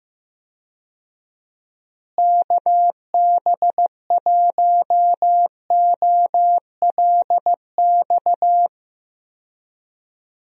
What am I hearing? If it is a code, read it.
Morse code, heard as a keyed tone, about 15 wpm, 700 Hz. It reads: KB1OLX
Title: USA Call Signs - 6 Characters - Repeated 2x - 15wpm